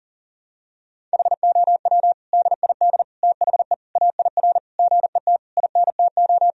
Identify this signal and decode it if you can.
Morse code, heard as a keyed tone, about 40 wpm, 700 Hz. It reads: HOW DID THE AIR GET INTO
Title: Sentences from Top 1000 Words - Review Part 1 - 40wpm